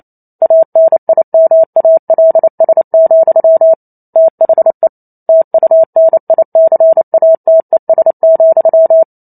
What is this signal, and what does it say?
Morse code, heard as a keyed tone, about 28 wpm, 650 Hz. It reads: ANIMALS, THE TUNICATES,